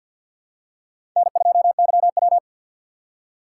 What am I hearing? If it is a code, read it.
Morse code, heard as a keyed tone, about 52 wpm, 700 Hz. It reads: N2YW
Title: USA Call Signs - 4 Characters - 2x Character Spacing - 50wpm